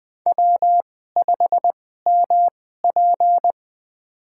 Morse code, heard as a keyed tone, about 20 wpm, spaced as for 15 wpm, 700 Hz. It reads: W5MP
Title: USA Call Signs - 4 Characters - 2x Character Spacing - 20wpm